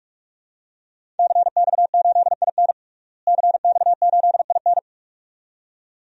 Morse code, heard as a keyed tone, about 44 wpm, 700 Hz. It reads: CX8IN CX8IN
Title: DXCC-100 Call Signs - 5 Characters - Repeated 2x - 45wpm